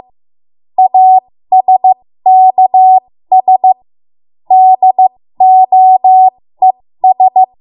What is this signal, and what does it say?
Morse code, heard as a keyed tone, about 15 wpm, spaced as for 13 wpm, 750 Hz. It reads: ASKS DOES
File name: Morse / 13WPM